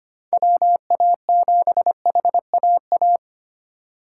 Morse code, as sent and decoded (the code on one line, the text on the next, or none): .-- .- --... .... .- .-
WA7HAA